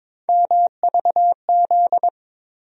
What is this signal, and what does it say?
Morse code, heard as a keyed tone, about 22 wpm, 700 Hz. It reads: MVZ
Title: USA Call Sign Postfixes - 3 Letters 22wpm